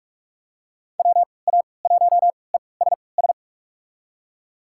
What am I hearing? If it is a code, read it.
Morse code, heard as a keyed tone, about 46 wpm, spaced as for 26 wpm, 700 Hz. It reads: WA1ESS
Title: USA Call Signs - 6 Characters - 3x Character Spacing - 45wpm